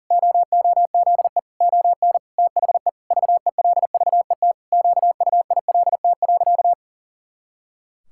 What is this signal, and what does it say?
Morse code, heard as a keyed tone, about 40 wpm, 700 Hz. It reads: OOZE ON THE VELVET QUILT.
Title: Pangram Challenge v2 - 40wpm